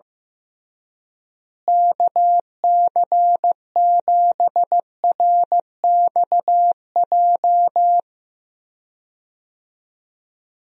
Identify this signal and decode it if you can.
Morse code, heard as a keyed tone, about 15 wpm, 700 Hz. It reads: KC7RXJ
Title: USA Call Signs - 6 Characters - Repeated 3x - 15wpm